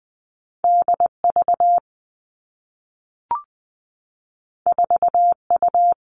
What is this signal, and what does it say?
Morse code, heard as a keyed tone, about 20 wpm, 700 Hz. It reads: D V 4 U
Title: Two Challenging Letter-Number Combinations 20wpm